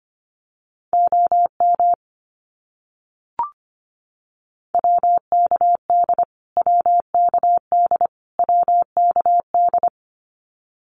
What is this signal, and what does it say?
Morse code, heard as a keyed tone, about 25 wpm, 700 Hz. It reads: OM WKD WKD WKD
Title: Abbreviations Repeated 3x - 25wpm